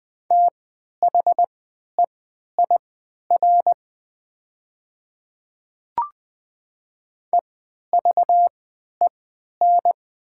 Morse code, heard as a keyed tone, about 20 wpm, spaced as for 11 wpm, 700 Hz. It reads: THEIR EVEN
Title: 3x Character Spacing - Top 200 Words - 20wpm